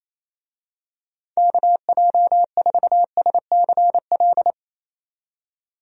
Morse code, heard as a keyed tone, about 27 wpm, 700 Hz. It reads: KJ4SCL
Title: USA Call Signs - 6 Characters - 28wpm